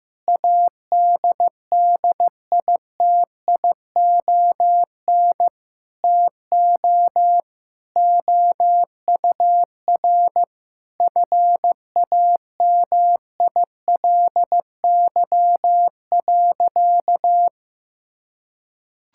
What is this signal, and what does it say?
Morse code, heard as a keyed tone, about 15 wpm, 700 Hz. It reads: ADDITION TO OUR FAMILY.
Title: Sentences from Top 1000 Words - Review Part 6 - 15wpm